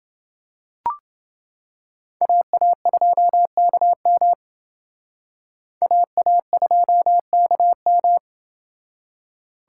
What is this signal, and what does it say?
Morse code, heard as a keyed tone, about 28 wpm, 700 Hz. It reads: AA2KM AA2KM